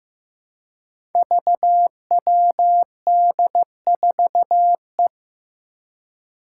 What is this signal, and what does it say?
Morse code, heard as a keyed tone, about 15 wpm, 700 Hz. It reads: VWD4E
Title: Five Challenging Letter-Number Combinations 15wpm